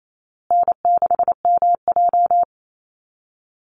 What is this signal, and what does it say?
Morse code, heard as a keyed tone, about 28 wpm, 700 Hz. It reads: N6MJ
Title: Easy Call Signs Repeated 3x - 28wpm